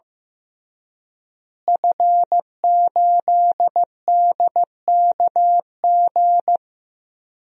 Morse code, heard as a keyed tone, about 15 wpm, 700 Hz. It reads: F8DKG
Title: DXCC-100 Call Signs - 5 Characters - Repeated 2x - 15wpm